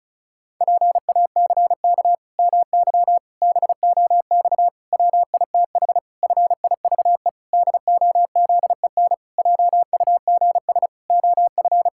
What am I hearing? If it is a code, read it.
Morse code, heard as a keyed tone, about 36 wpm, 700 Hz. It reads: PACK MY BOX WITH FIVE DOZEN JUGS OF
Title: Pangram Challenge v2 - 35wpm